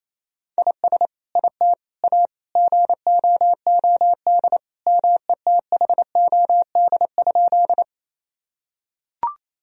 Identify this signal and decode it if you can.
Morse code, heard as a keyed tone, about 28 wpm, 700 Hz. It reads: IS IT A GOOD METHOD?
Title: Sentences from Top 1000 Words - Review Part 3 - 28wpm